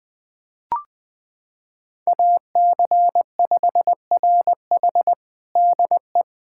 Morse code, heard as a keyed tone, about 20 wpm, 700 Hz. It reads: AC5RH DE
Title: USA Callsign DE Callsign 20wpm